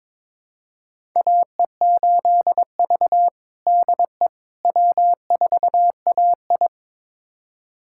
Morse code, heard as a keyed tone, about 22 wpm, 700 Hz. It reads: AE8V DE W4AI
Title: USA Callsign DE Callsign - 4 Characters - 22wpm